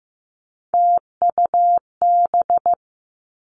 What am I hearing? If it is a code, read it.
Morse code, heard as a keyed tone, about 15 wpm, 700 Hz. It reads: TUB
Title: Three Challenging Letter-Number Combinations 15wpm